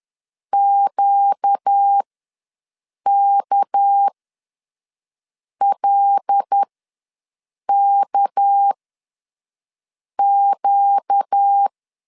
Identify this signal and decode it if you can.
Morse code, heard as a keyed tone, about 10 wpm, spaced as for 5 wpm, 800 Hz. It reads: QKLKQ
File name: LEZIONI / 6 PKX0 / 6 PK ws 10wpm 3745